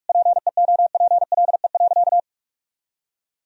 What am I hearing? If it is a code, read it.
Morse code, heard as a keyed tone, about 41 wpm, 700 Hz. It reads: PEOPLE.
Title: Sentences from Top 1000 Words - Review Part 3 - 45wpm